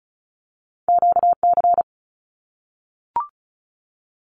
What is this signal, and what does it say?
Morse code, heard as a keyed tone, about 36 wpm, 700 Hz. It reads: QC
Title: Canadian Provinces and Territories 35wpm